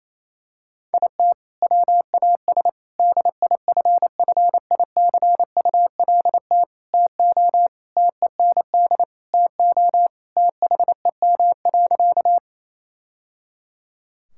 Morse code, heard as a keyed tone, about 28 wpm, 700 Hz. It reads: IT WAS DIFFICULT TO TEND TO THEM.
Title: Sentences from Top 1000 Words - Review Part 5 - 28wpm